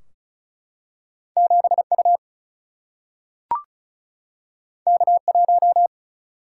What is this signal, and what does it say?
Morse code, heard as a keyed tone, about 36 wpm, 700 Hz. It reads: 7U K1
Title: Single-Letter Single-Number 35wpm